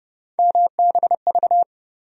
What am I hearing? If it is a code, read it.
Morse code, heard as a keyed tone, about 32 wpm, 700 Hz. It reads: MBV
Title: USA Call Sign Postfixes - 3 Letters 30wpm